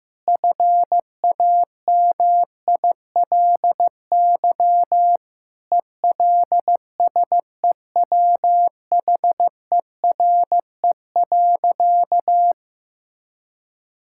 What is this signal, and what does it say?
Morse code, heard as a keyed tone, about 15 wpm, 700 Hz. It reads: FAMILY ELSEWHERE.
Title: Sentences from Top 1600 Words 15wpm